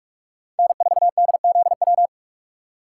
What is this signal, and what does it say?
Morse code, heard as a keyed tone, about 47 wpm, 700 Hz. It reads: N4DZW
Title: USA Call Signs - 5 Characters - 6x Character Spacing - 45wpm